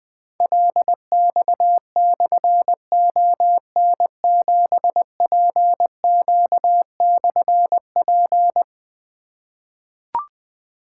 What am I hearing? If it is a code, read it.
Morse code, heard as a keyed tone, about 20 wpm, 700 Hz. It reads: LX/ON7PQ/P
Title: Difficult Call Signs 20wpm